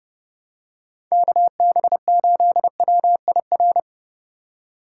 Morse code, heard as a keyed tone, about 30 wpm, 700 Hz. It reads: KB8WIR